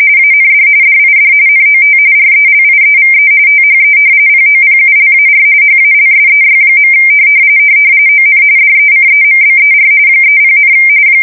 RTTY, about 45.45 baud. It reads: AND DID NOT RELIEVE THE POPULAR HALSEY. ON DECEMBER 30,
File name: fsk170 / battleshipnj